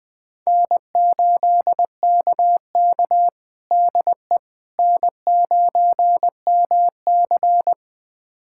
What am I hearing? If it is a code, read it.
Morse code, heard as a keyed tone, about 20 wpm, 700 Hz. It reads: N8KK DE N9MC
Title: USA Callsign DE Callsign - 4 Characters - 20wpm